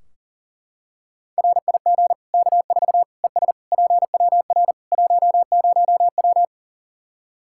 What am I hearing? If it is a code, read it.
Morse code, heard as a keyed tone, about 40 wpm, 700 Hz. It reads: RIG K4 ES PWR 10W
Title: Practice QSO - Exchange 2 - Info - 40wpm